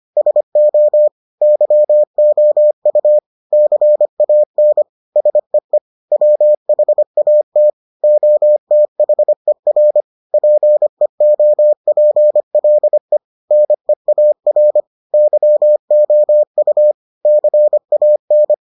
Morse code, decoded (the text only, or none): SO YOU CAN SEE WHAT OTHER PEOPLE NEAR YOU CAN